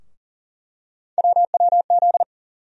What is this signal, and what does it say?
Morse code, heard as a keyed tone, about 40 wpm, 700 Hz. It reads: WWZ
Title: USA Call Sign Postfixes - 3 Letters 40wpm